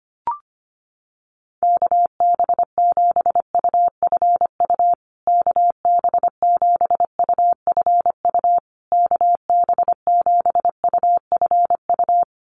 Morse code, heard as a keyed tone, about 25 wpm, 700 Hz. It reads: KB7UFU KB7UFU KB7UFU
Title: USA Call Signs Repeated 3x 25wpm